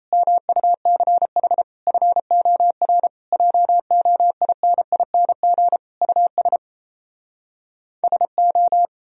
Morse code, heard as a keyed tone, about 38 wpm, 700 Hz. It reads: MUCH FOR JOINING US SO